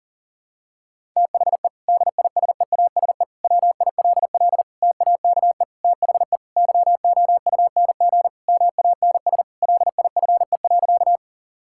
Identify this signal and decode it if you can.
Morse code, heard as a keyed tone, about 40 wpm, 700 Hz. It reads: THE DISEASE WILL TAKE THE YOUNG MANS LIFE.
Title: Sentences from Top 1000 Words - Review Part 6 - 40wpm